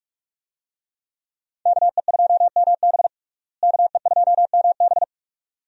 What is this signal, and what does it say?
Morse code, heard as a keyed tone, about 45 wpm, 700 Hz. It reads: KE2MB KE2MB
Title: USA Call Signs - 5 Characters - Repeated 2x - 45wpm